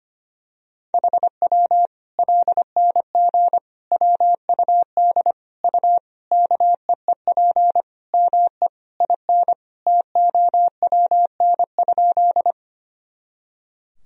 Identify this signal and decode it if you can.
Morse code, heard as a keyed tone, about 25 wpm, 700 Hz. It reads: HW LNG WUD U KEEP ME IN TOWN?
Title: Sentences from Top 500 Words and Abbreviations 25wpm